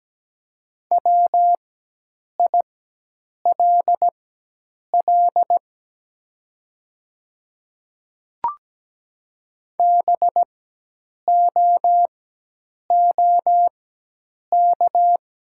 Morse code, heard as a keyed tone, about 17 wpm, spaced as for 8 wpm, 700 Hz. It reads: WILL BOOK
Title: 4x Character Spacing - Top 300 Words - 17wpm